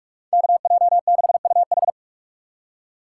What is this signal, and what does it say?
Morse code, heard as a keyed tone, about 44 wpm, 700 Hz. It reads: KJ6UH